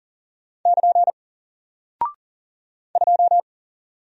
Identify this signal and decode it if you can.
Morse code, heard as a keyed tone, about 40 wpm, 700 Hz. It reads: <KN> 2
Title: Single Character (Letter:Number:Punctuation:Prosign) 40wpm